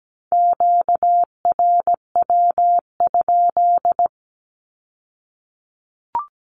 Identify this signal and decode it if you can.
Morse code, heard as a keyed tone, about 17 wpm, 700 Hz. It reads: QRW?
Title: Q-Codes with Spoken Meaning - 17wpm